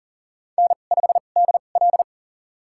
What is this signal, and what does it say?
Morse code, heard as a keyed tone, about 40 wpm, spaced as for 29 wpm, 700 Hz. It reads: N5DL
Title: USA Call Signs - 2x Character Spacing - 40wpm